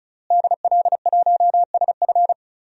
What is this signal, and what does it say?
Morse code, heard as a keyed tone, about 36 wpm, 700 Hz. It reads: DL1SF
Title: DXCC-100 Call Signs - 5 Characters - Repeated 2x - 35wpm